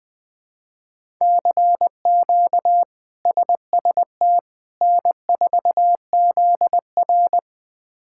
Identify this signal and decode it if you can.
Morse code, heard as a keyed tone, about 20 wpm, 700 Hz. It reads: CQ SST N4ZR
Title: K1USN SST QSO Practice - 20wpm